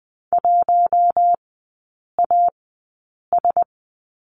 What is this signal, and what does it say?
Morse code, heard as a keyed tone, about 20 wpm, spaced as for 8 wpm, 700 Hz. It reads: 1AS
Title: 053 Lesson 10e - Letter r - Three Characters 20wpm